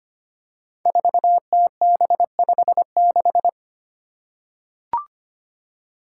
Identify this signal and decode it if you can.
Morse code, heard as a keyed tone, about 25 wpm, 700 Hz. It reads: 4TB56
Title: Five Challenging Letter-Number Combinations 25wpm